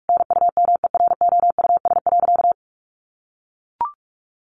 Morse code, heard as a keyed tone, about 44 wpm, 700 Hz. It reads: NUMEROUS.
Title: Sentences from Top 1400 Words 45wpm